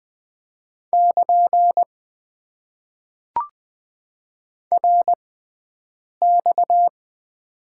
Morse code, heard as a keyed tone, about 20 wpm, spaced as for 7 wpm, 700 Hz. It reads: <KN> RX
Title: Top 90 QSO Elements - 6x Character Spacing - 20wpm